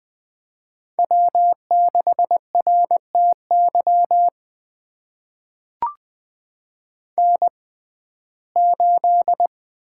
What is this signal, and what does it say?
Morse code, heard as a keyed tone, about 20 wpm, spaced as for 14 wpm, 700 Hz. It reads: W6RTY N8
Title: USA Call Signs - 5 Characters - 6x Character Spacing - 20wpm